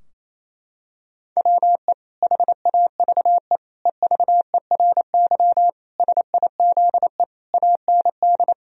Morse code, heard as a keyed tone, about 28 wpm, 700 Hz. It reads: WE HAVE EVERY SIZE AND
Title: Sentences from Top 1000 Words - Review Part 2 - 28wpm